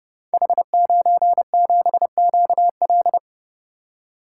Morse code, heard as a keyed tone, about 30 wpm, 700 Hz. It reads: H97QL
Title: Code Groups - 5 Letter:Numbers - 30wpm